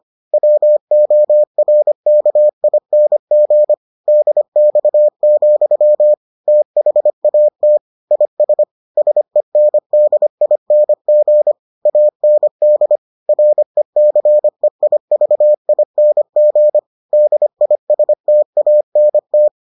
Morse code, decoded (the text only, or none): WORKING DX, THAT IS SENDING AND RECEIVING DISTANT